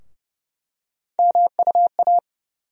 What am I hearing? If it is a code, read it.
Morse code, heard as a keyed tone, about 30 wpm, 700 Hz. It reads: MUA
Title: USA Call Sign Postfixes - 3 Letters 30wpm